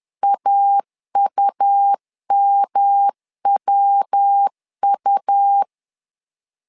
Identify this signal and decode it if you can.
Morse code, heard as a keyed tone, about 10 wpm, 800 Hz. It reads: AUMWU
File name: LEZIONI / 3 MWRO / 3 MW 10wpm 1959